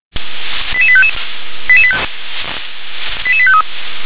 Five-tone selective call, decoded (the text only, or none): 88855 88853